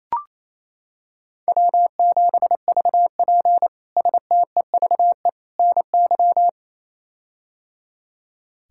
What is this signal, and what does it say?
Morse code, heard as a keyed tone, about 27 wpm, 700 Hz. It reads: W7VP STEVE NY
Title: K1USN SST QSO Practice - 28wpm